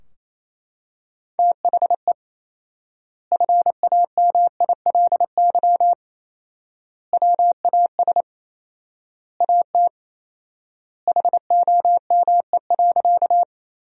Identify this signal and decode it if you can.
Morse code, heard as a keyed tone, about 28 wpm, 700 Hz. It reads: THE FAMILY WAS AT HOME.
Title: 4x Word-Spacing - Sentences from Top 400 Words - 28wpm